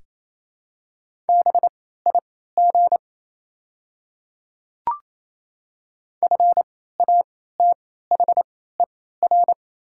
Morse code, heard as a keyed tone, about 28 wpm, spaced as for 16 wpm, 700 Hz. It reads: BIG FATHER
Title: 3x Character Spacing - Top 200 Words - 28wpm